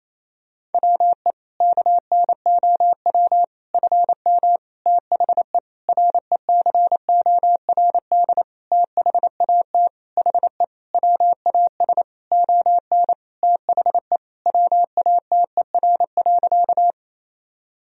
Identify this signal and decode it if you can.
Morse code, heard as a keyed tone, about 28 wpm, 700 Hz. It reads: WE KNOW FM THE RECORD THAT HE WAS ON THE WATER.
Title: Sentences from Top 500 Words and Abbreviations 28wpm